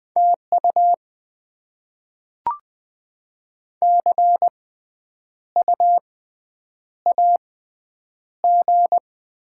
Morse code, heard as a keyed tone, about 20 wpm, spaced as for 9 wpm, 690 Hz. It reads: TU CU AG
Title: Top 20 QSO Elements - 6x Character Spacing - 20wpm